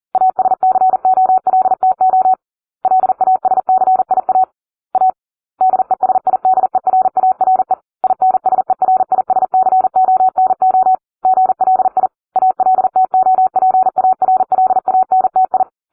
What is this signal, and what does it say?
Morse code, heard as a keyed tone, about 40 wpm, 750 Hz. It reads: ASCOLTO LASCIA A DESIDERARE INSERISCONO GLI ALTOPARLANTI